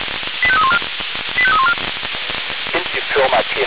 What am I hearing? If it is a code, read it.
Five-tone selective call (ZVEI), heard as quick runs of five tones, tones of about 70 ms, with speech. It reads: 85314 85314